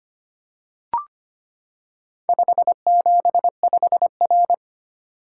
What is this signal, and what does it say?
Morse code, heard as a keyed tone, about 25 wpm, 700 Hz. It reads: 575R
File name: practice / 25wpm / Four Letter-Number Combinations - 25wpm